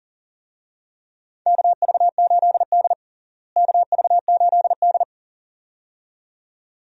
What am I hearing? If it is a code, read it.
Morse code, heard as a keyed tone, about 40 wpm, 700 Hz. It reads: KV8D KV8D
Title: USA Call Signs - 4 Characters - Repeated 2x - 40wpm